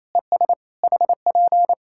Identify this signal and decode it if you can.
Morse code, heard as a keyed tone, about 28 wpm, 700 Hz. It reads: ES HP
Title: Practice QSO - Exchange 4 - Ending - 28wpm